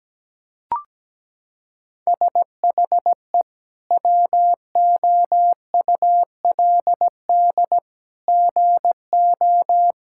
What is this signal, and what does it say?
Morse code, heard as a keyed tone, about 17 wpm, 700 Hz. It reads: SHE WOULD GO
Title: Sentences from Top 1600 Words 17wpm